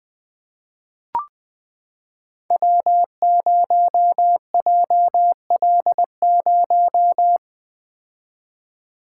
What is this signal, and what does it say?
Morse code, heard as a keyed tone, about 20 wpm, 700 Hz. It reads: W0JL0
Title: Code Groups - 5 Letter:Numbers - 20wpm